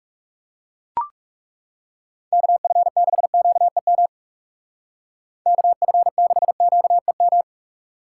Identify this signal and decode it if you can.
Morse code, heard as a keyed tone, about 36 wpm, 700 Hz. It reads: KF6QEM KF6QEM
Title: Speed Racing - USA 2x3 Call Signs 30wpm